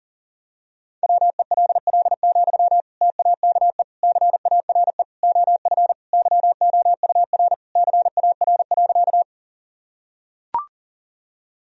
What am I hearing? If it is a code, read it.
Morse code, heard as a keyed tone, about 40 wpm, 700 Hz. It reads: WELL, TAKE CARE OF YOUR CAR.
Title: Sentences from Top 1000 Words - Review Part 1 - 40wpm